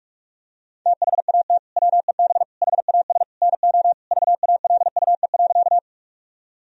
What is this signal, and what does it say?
Morse code, heard as a keyed tone, about 41 wpm, 700 Hz. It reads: THAT WEB HAS NO VALUE.
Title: Sentences from Top 1000 Words - Review Part 4 - 45wpm